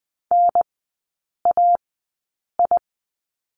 Morse code, heard as a keyed tone, about 20 wpm, spaced as for 8 wpm, 700 Hz. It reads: NAI
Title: 041 Lesson 8e - Number 1 - Three Characters 20wpm